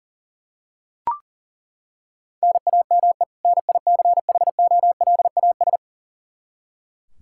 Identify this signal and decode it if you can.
Morse code, heard as a keyed tone, about 40 wpm, 700 Hz. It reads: NAME NICHOLAS
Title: K1USN SST QSO Practice 40wpm